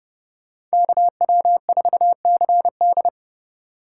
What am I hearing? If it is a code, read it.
Morse code, heard as a keyed tone, about 30 wpm, 700 Hz. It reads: KW4CD